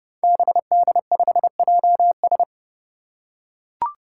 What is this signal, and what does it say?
Morse code, heard as a keyed tone, about 31 wpm, 700 Hz. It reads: BD5JS
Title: Five Challenging Letter-Number Combinations 30wpm